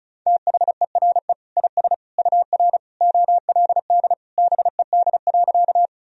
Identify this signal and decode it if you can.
Morse code, heard as a keyed tone, about 36 wpm, 700 Hz. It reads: THERE IS UR OLD BED.
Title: Sentences from Top 500 Words and Abbreviations 35wpm